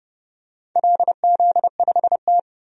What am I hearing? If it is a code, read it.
Morse code, heard as a keyed tone, about 30 wpm, 700 Hz. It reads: LZ5T
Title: DXCC-100 Call Signs - 4 Characters - 30wpm